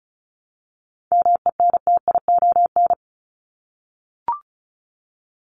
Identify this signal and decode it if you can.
Morse code, heard as a keyed tone, about 34 wpm, 700 Hz. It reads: MENTION
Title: Top 1501-1750 Words 35wpm